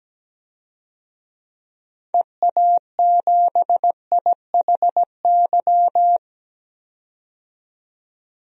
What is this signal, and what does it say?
Morse code, heard as a keyed tone, about 17 wpm, 700 Hz. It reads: EA7IHY
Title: DXCC-100 Call Signs - 6 Characters - 17wpm